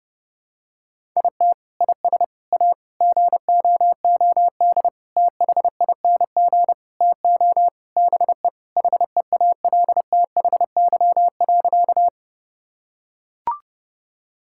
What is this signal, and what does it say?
Morse code, heard as a keyed tone, about 30 wpm, 700 Hz. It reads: IT IS A GOOD THING TO BE HEALTHY.
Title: Sentences from Top 1000 Words - Review Part 5 - 30wpm